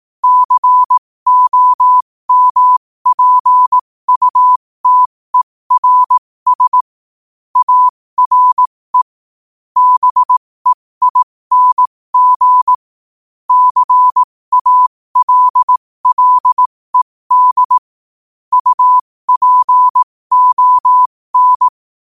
Morse code, heard as a keyed tone, about 18 wpm, spaced as for 15 wpm, 1000 Hz. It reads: COMPUTERS ARE BEING CALLED UPON
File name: qotdfiles / 15wpm